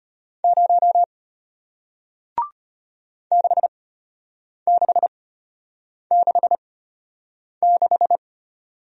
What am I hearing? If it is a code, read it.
Morse code, heard as a keyed tone, about 33 wpm, spaced as for 8 wpm, 700 Hz. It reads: 0 6666